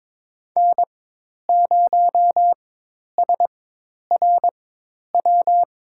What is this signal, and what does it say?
Morse code, heard as a keyed tone, about 22 wpm, spaced as for 10 wpm, 700 Hz. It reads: N0SRW
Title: USA Call Signs - 5 Characters - 4x Character Spacing - 22wpm